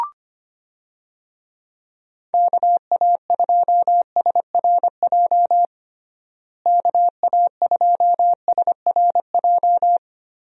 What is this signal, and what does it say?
Morse code, heard as a keyed tone, about 25 wpm, 700 Hz. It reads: KA2SRJ KA2SRJ
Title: USA Call Signs - 6 Characters - Repeated 3x - 25wpm